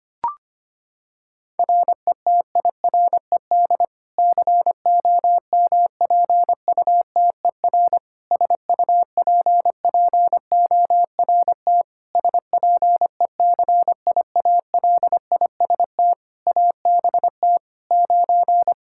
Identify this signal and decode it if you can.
Morse code, heard as a keyed tone, about 25 wpm, 700 Hz. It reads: RETIRED COMPUTER SUPPORT SPECIALIST ABT 9
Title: Practice QSO - Exchange 3 - More Info - 25wpm